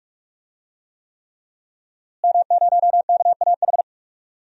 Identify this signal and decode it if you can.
Morse code, heard as a keyed tone, about 41 wpm, 700 Hz. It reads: M0KAH
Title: DXCC-100 Call Signs - 5 Characters - Repeated 3x - 45wpm